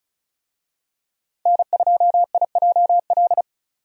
Morse code, heard as a keyed tone, about 36 wpm, 700 Hz. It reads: N2IJL
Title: USA Call Signs - 5 Characters - 35wpm